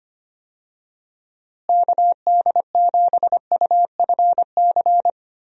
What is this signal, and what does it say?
Morse code, heard as a keyed tone, about 25 wpm, 700 Hz. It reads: KD7UFC
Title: USA Call Signs - 6 Characters - Repeated 3x - 25wpm